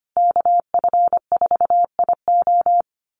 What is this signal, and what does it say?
Morse code, heard as a keyed tone, about 25 wpm, 700 Hz. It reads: KF4IO
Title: USA Call Signs Repeated 2x 25wpm